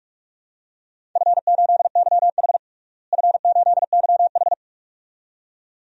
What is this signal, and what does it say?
Morse code, heard as a keyed tone, about 46 wpm, 700 Hz. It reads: F8YH F8YH
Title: DXCC-100 Call Signs - 4 Characters - Repeated 2x - 45wpm